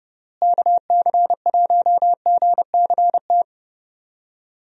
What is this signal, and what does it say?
Morse code, heard as a keyed tone, about 30 wpm, 700 Hz. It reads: KC1GCT
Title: USA Call Signs - 6 Characters - Repeated 3x - 30wpm